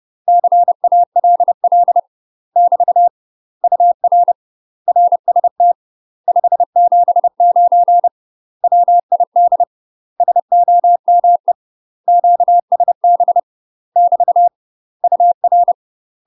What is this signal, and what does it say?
Morse code, heard as a keyed tone, about 30 wpm, 700 Hz. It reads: CALL <BT> UR RST 579 WID SOME QSB <BT> UR
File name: UK MASTER QSOs TRAINING / 30WPM / UK MASTER QSOs 0031